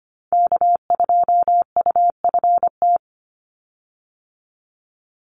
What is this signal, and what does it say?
Morse code, heard as a keyed tone, about 25 wpm, 700 Hz. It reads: K2UFT
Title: Call Signs Repeated 3x - 25wpm